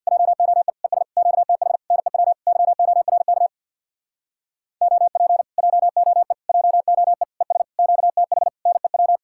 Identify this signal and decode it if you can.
Morse code, heard as a keyed tone, about 67 wpm, 700 Hz. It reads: JOE ES QTH NEW YORK OP JOE JOE ES QTH NEW